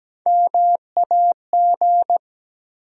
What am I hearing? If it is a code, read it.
Morse code, heard as a keyed tone, about 17 wpm, 700 Hz. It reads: MAG
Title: USA Call Sign Postfixes - 3 Letters 17wpm